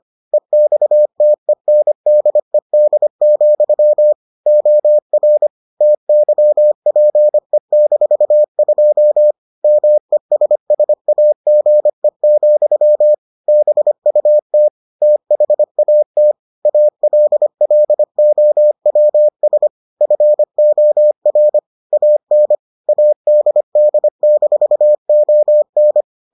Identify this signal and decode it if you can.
Morse code, heard as a keyed tone, about 25 wpm, 600 Hz. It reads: EXTENDED, OR TYPE-2 MESSAGE, BUT THAT ALLOWS FOR AN ADD-ON